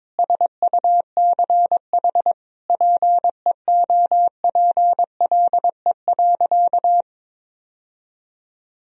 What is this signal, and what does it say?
Morse code, heard as a keyed tone, about 21 wpm, 700 Hz. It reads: SUCH PEOPLE.
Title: Sentences from Top 1000 Words - Review Part 3 - 22wpm